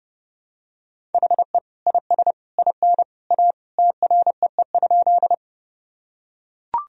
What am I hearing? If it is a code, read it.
Morse code, heard as a keyed tone, about 32 wpm, 700 Hz. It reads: HE IS IN A TREE?
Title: Sentences from Top 1000 Words - Review Part 1 - 30wpm